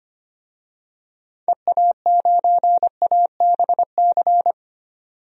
Morse code, heard as a keyed tone, about 25 wpm, 700 Hz. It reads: EA9ABC
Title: DXCC-100 Call Signs - 6 Characters - 25wpm